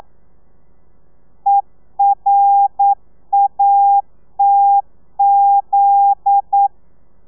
Morse code, heard as a keyed tone, about 9 wpm, 800 Hz. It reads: ERATZ
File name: telegrafia / i2tfj / Lez11 40